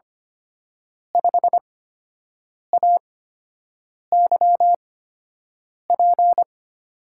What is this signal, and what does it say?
Morse code, heard as a keyed tone, about 25 wpm, spaced as for 7 wpm, 700 Hz. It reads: 5AYP
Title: USA Call Signs - 6 Characters - 8x Character Spacing - 25wpm